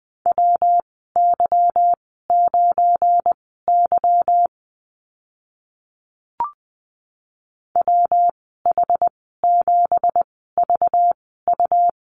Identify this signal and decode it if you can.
Morse code, heard as a keyed tone, about 20 wpm, spaced as for 14 wpm, 700 Hz. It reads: WY9Y WH7VU
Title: 231 Lesson 37g - Letter x - Call Signs 20wpm